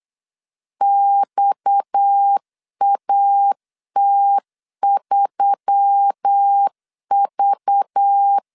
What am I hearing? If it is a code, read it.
Morse code, heard as a keyed tone, about 8 wpm, 800 Hz. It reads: XAT3V
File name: LEZIONI / 6 PKX0 / 6 PKX 8wpm 3041